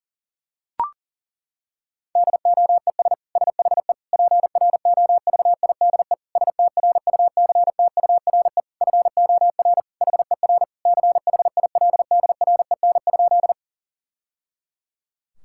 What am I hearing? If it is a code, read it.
Morse code, heard as a keyed tone, about 40 wpm, 700 Hz. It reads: DOES SHE PROVIDE STRUCTURE FOR HER CHILDREN?
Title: Sentences from Top 1000 Words - Review Part 6 - 40wpm